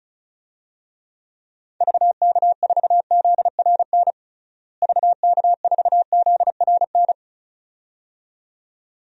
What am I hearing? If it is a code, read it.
Morse code, heard as a keyed tone, about 36 wpm, 700 Hz. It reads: VK4ZRN VK4ZRN
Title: DXCC-100 Call Signs - 6 Characters - Repeated 2x - 35wpm